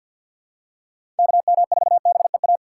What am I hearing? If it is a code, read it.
Morse code, heard as a keyed tone, about 49 wpm, 700 Hz. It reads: KM4BEA
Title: USA Call Signs - 6 Characters - 6x Character Spacing - 50wpm